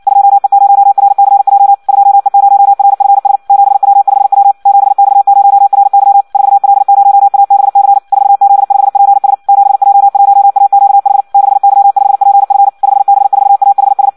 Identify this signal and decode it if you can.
Morse code, heard as a keyed tone, about 56 wpm, 800 Hz. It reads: 2E0RCJ 2E0RFS 7W5W 7X0RY 4L0ABC 4Z5PS 7J1AQH 6Y5PF 5B4AHS
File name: Audio practica CW / qrq-tx-58 2